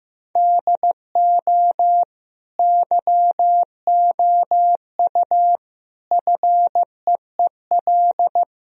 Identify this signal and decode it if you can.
Morse code, heard as a keyed tone, about 16 wpm, 700 Hz. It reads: DO YOU FEEL
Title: Sentences from Top 1000 Words - Review Part 3 - 15wpm